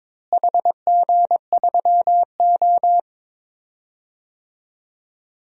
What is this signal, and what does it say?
Morse code, heard as a keyed tone, about 22 wpm, 700 Hz. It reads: HG3O
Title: DXCC-100 Call Signs - 4 Characters - Repeated 3x - 22wpm